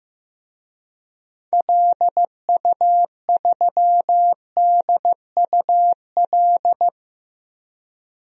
Morse code, heard as a keyed tone, about 15 wpm, 700 Hz. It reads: LU3DUL
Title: DXCC-100 Call Signs - 6 Characters - 15wpm